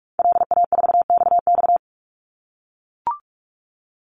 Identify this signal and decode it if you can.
Morse code, heard as a keyed tone, about 44 wpm, 700 Hz. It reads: LA4XX
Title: Call Signs - 45wpm